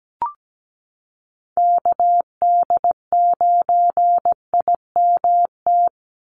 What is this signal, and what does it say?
Morse code, heard as a keyed tone, about 17 wpm, 700 Hz. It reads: KD9IMT
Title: USA Call Signs 17wpm